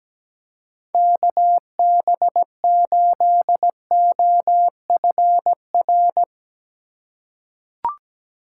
Morse code, heard as a keyed tone, about 17 wpm, 700 Hz. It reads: KB8OFR
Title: USA Call Signs - 6 Characters - 4x Character Spacing - 17wpm